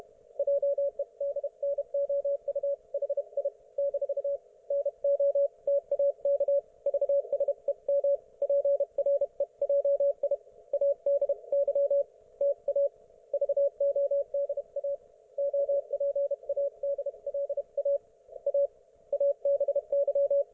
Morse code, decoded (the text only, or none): JEDNOU HI - NOTAK VSEM PREJI ADY TA VODA OPADLA A ABY